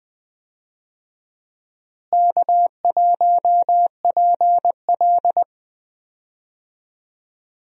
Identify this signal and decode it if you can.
Morse code, heard as a keyed tone, about 21 wpm, 700 Hz. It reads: K1PL
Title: USA Call Signs - 4 Characters - Repeated 2x - 20wpm